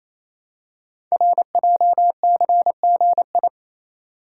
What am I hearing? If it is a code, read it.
Morse code, heard as a keyed tone, about 28 wpm, 700 Hz. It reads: RJCGI
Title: Code Groups - 5 Letters - 28wpm